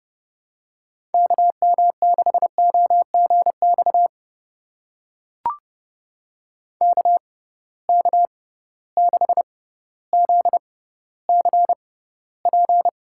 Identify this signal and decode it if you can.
Morse code, heard as a keyed tone, about 30 wpm, spaced as for 15 wpm, 700 Hz. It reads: KM6OGX KK6ZCP
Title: USA Call Signs - 6 Characters - 6x Character Spacing - 30wpm